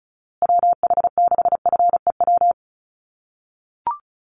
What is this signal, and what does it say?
Morse code, heard as a keyed tone, about 34 wpm, 700 Hz. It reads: WH6FEW
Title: USA Call Signs Repeated 3x 35wpm